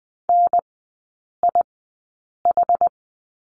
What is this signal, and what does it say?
Morse code, heard as a keyed tone, about 20 wpm, spaced as for 8 wpm, 700 Hz. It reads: NIH
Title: 093 Lesson 16e - Letter c - Three Characters 20wpm